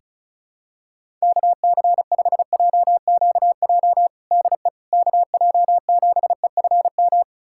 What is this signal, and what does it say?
Morse code, heard as a keyed tone, about 36 wpm, 700 Hz. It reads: KC5JQJ DE KJ7EFM
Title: USA Callsign DE Callsign - 6 Characters - 35wpm